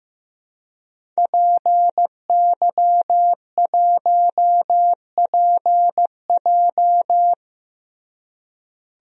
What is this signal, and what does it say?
Morse code, heard as a keyed tone, about 15 wpm, 700 Hz. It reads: PY1PJ
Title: DXCC-100 Call Signs - 5 Characters - Repeated 3x - 15wpm